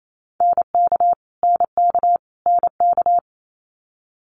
Morse code, heard as a keyed tone, about 28 wpm, 700 Hz. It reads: NK NK NK
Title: Multi-letter Phonemes Repeated 3x 28wpm